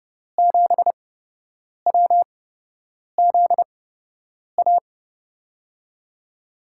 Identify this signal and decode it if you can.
Morse code, heard as a keyed tone, about 30 wpm, spaced as for 8 wpm, 700 Hz. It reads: 7WZA
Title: USA Call Signs - 6 Characters - 8x Character Spacing - 30wpm